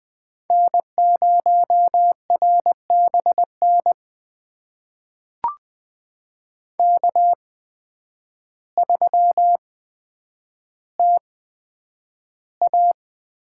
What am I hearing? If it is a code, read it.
Morse code, heard as a keyed tone, about 20 wpm, spaced as for 9 wpm, 700 Hz. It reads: N0RBN K3TA
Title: USA Call Signs - 8x Character Spacing - 20wpm